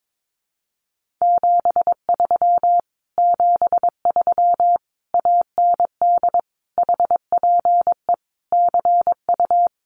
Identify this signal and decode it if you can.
Morse code, heard as a keyed tone, about 22 wpm, 700 Hz. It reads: 73 73 AND HPE CU
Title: Common QSO Phrases Encore 22wpm